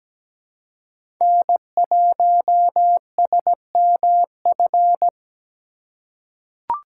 Morse code, heard as a keyed tone, about 17 wpm, 700 Hz. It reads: N1SMF
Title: USA Call Signs - 5 Characters - 6x Character Spacing - 17wpm